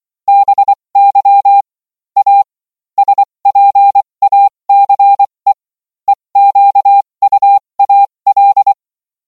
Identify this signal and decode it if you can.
Morse code, heard as a keyed tone, about 24 wpm, spaced as for 20 wpm, 800 Hz. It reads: BY A SPACE EQUAL